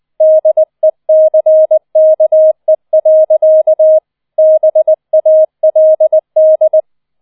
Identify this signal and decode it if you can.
Morse code, heard as a keyed tone, about 20 wpm, 630 Hz. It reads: DECKE. BALD
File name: 20wpm